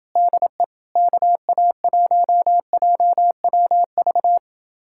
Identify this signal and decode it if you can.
Morse code, heard as a keyed tone, about 28 wpm, 700 Hz. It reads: DE KA1JWV